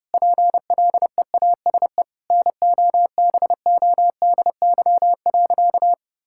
Morse code, heard as a keyed tone, about 30 wpm, 700 Hz. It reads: PLEASE NOBODY.
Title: Sentences from Top 1600 Words 30wpm